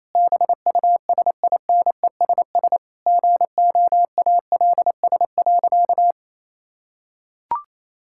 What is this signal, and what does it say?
Morse code, heard as a keyed tone, about 28 wpm, 700 Hz. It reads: BUSINESS GOALS.
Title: Sentences from Top 1000 Words - Review Part 6 - 28wpm